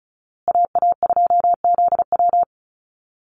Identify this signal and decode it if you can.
Morse code, heard as a keyed tone, about 34 wpm, 700 Hz. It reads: AA2ZW
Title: Call Signs Repeated 2x - 35wpm